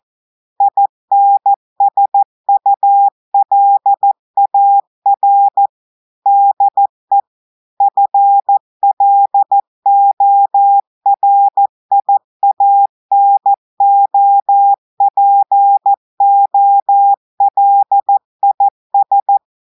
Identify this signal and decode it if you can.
Morse code, heard as a keyed tone, about 14 wpm, 800 Hz. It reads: INSULAR DE FLORIANOPOLIS
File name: audios / portugais / portugais texte en clair 1 vitesse 14 mots minutes